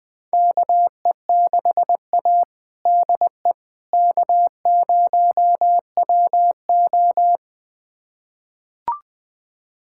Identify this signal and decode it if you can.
Morse code, heard as a keyed tone, about 20 wpm, 700 Hz. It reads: KE6A DE K0WO
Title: USA Callsign DE Callsign - 4 Characters - 20wpm